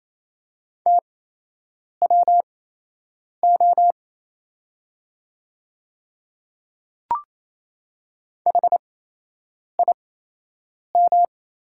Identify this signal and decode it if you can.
Morse code, heard as a keyed tone, about 28 wpm, spaced as for 8 wpm, 700 Hz. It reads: TWO HIM
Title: 8x Character Spacing - Top 100 Words - 28wpm